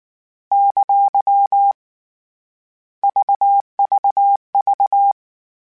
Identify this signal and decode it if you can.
Morse code, heard as a keyed tone, about 19 wpm, 800 Hz. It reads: ! VVV